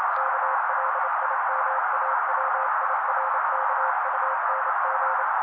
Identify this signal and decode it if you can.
Morse code, heard as a keyed tone, about 28 wpm, 530 Hz. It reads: KLIMAWIRKUNG
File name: daily / 27wpm noise